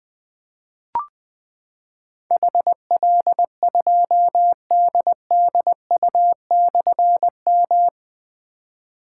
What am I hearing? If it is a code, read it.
Morse code, heard as a keyed tone, about 20 wpm, 700 Hz. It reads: HL2DDU/M
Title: Difficult Call Signs 20wpm